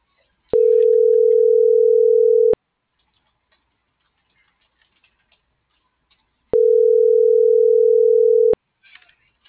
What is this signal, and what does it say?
Ringback tone (North America), 440 + 480 Hz.
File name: CENTRAL MAERSK / out isabel.bravo - .76005